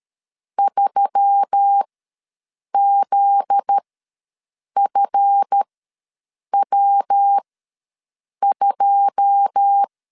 Morse code, heard as a keyed tone, about 13 wpm, spaced as for 7 wpm, 800 Hz. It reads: 3ZFW2
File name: LEZIONI / 9 2 ws 12wpm 3406